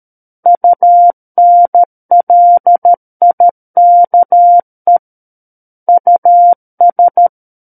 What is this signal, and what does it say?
Morse code, heard as a keyed tone, about 13 wpm, 700 Hz. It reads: UNLIKE US
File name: Book One / 04-Chapter 4 - 13 WPM